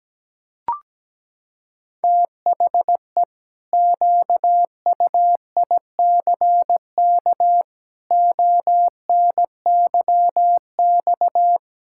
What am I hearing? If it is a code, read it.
Morse code, heard as a keyed tone, about 17 wpm, 700 Hz. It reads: THE QUICK ONYX
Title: Pangram Challenge v2 - 17wpm